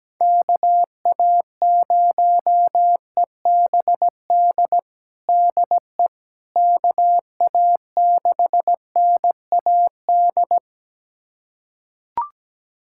Morse code, heard as a keyed tone, about 17 wpm, 700 Hz. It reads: KA0EBD DE KA6NAD
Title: USA Callsign DE Callsign - 6 Characters - 17wpm